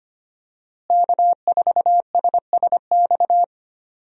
Morse code, heard as a keyed tone, about 25 wpm, 700 Hz. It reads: K4SSX
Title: USA Call Signs - 5 Characters - Repeated 3x - 25wpm